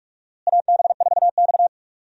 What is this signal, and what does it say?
Morse code, heard as a keyed tone, about 46 wpm, 700 Hz. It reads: AB4X
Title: USA Call Signs - 4 Characters - Repeated 3x - 45wpm